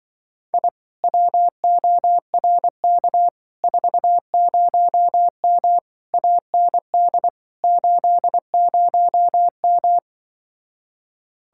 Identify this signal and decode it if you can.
Morse code, heard as a keyed tone, about 24 wpm, 700 Hz. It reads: I WORK 40M AND 80M
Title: Speed-Racing Common QSO Phrases - 20wpm